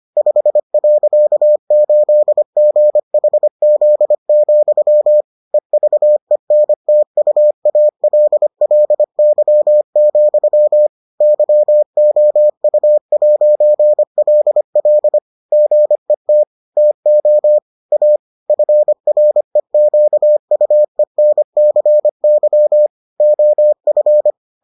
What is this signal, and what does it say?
Morse code, heard as a keyed tone, about 25 wpm, 600 Hz. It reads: 5.8GHZ, EVENTUALLY, YOU'LL GET TO A FREQUENCY OF